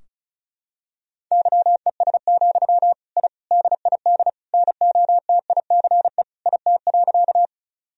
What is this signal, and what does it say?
Morse code, heard as a keyed tone, about 36 wpm, 700 Hz. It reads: YES, I DID NOTICE IT.
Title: Sentences from Top 1000 Words - Review Part 2 - 35wpm